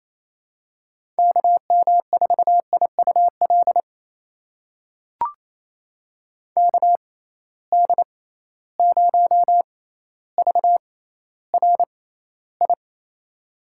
Morse code, heard as a keyed tone, about 28 wpm, spaced as for 14 wpm, 700 Hz. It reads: KM4IUL KD0VRI
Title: USA Call Signs - 6 Characters - 6x Character Spacing - 28wpm